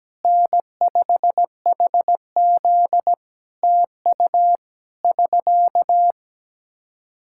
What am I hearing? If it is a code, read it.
Morse code, heard as a keyed tone, about 17 wpm, 700 Hz. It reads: N5HZ TU <SK>
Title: Practice QSO - Exchange 4 - Ending - 17wpm